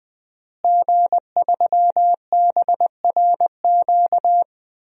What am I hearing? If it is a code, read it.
Morse code, heard as a keyed tone, about 20 wpm, 700 Hz. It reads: G3BRQ
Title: DXCC-100 Call Signs - 5 Characters - Repeated 2x - 20wpm